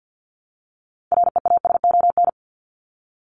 Morse code, heard as a keyed tone, about 50 wpm, 700 Hz. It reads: REASON
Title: ES:S Suffix - Top 100 Words with Base Words 50wpm